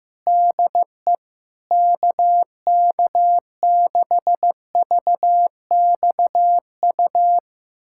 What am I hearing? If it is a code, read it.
Morse code, heard as a keyed tone, about 16 wpm, 700 Hz. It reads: DE KK6VXU
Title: USA Callsign DE Callsign - 6 Characters - 15wpm